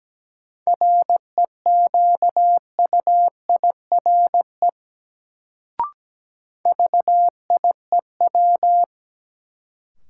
Morse code, heard as a keyed tone, about 17 wpm, 700 Hz. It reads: REQUIRE VIEW
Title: Sentences from Top 900 Words - New Word Drill - 17wpm